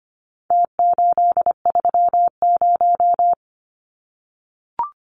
Morse code, heard as a keyed tone, about 25 wpm, 700 Hz. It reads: T830
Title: Numbers - Contest Style - 4-Digit 25wpm